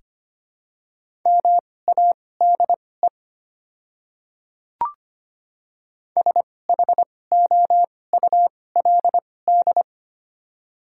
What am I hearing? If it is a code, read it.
Morse code, heard as a keyed tone, about 25 wpm, spaced as for 18 wpm, 700 Hz. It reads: MADE SHOULD
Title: 2x Character Spacing - Top 300 Words - 25wpm